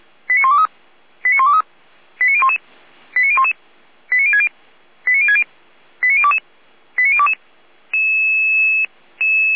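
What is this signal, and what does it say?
Five-tone selective call (ZVEI), heard as quick runs of five tones, tones of about 80 ms. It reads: 89123 89123 89011 89011 89060 89060 89022 89022
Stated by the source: ZVEI five-tone selective call 89163 received on an SDR.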